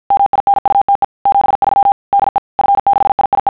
Morse code, heard as a keyed tone, about 50 wpm, 800 Hz. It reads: MINUTE 73 DE F6IIE
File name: sons / TEST50